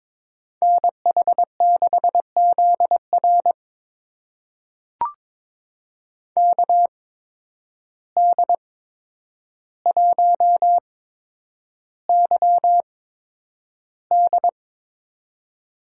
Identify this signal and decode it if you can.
Morse code, heard as a keyed tone, about 22 wpm, spaced as for 9 wpm, 700 Hz. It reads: NH6ZR KD1YD
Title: USA Call Signs - 5 Characters - 8x Character Spacing - 22wpm